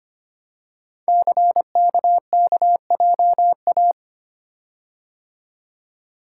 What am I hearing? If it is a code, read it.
Morse code, heard as a keyed tone, about 25 wpm, 700 Hz. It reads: CKKJA
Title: Code Groups - 5 Letters - 25wpm